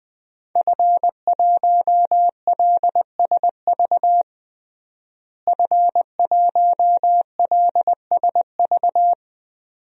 Morse code, heard as a keyed tone, about 21 wpm, 700 Hz. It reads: F1LSV F1LSV
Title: DXCC-100 Call Signs - 5 Characters - Repeated 2x - 20wpm